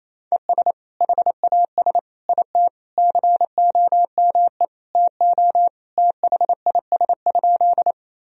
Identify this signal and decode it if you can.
Morse code, heard as a keyed tone, about 28 wpm, 700 Hz. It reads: ES HAS IT COME TO THIS?
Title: Sentences from Top 100 Words and Abbreviations 28wpm